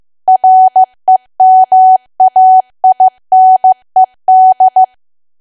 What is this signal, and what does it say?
Morse code, heard as a keyed tone, about 15 wpm, 750 Hz. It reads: REMAINED
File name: cwa / W1AW-Level2 / 15WPM